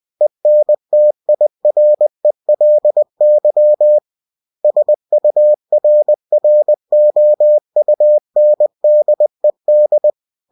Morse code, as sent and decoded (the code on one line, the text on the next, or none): . -. - .. .-. .-.. -.-- / ... ..- .-. .-. --- ..- -. -.. -..
ENTIRELY SURROUNDED